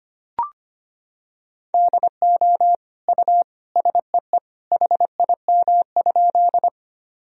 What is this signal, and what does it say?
Morse code, heard as a keyed tone, about 25 wpm, 700 Hz. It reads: DO U SEE HIM?
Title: Sentences from Top 100 Words and Abbreviations 25wpm